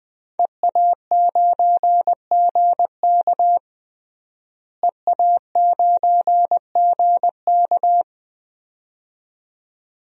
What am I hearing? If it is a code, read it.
Morse code, heard as a keyed tone, about 20 wpm, 700 Hz. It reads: EA9GK EA9GK
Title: DXCC-100 Call Signs - 5 Characters - Repeated 2x - 20wpm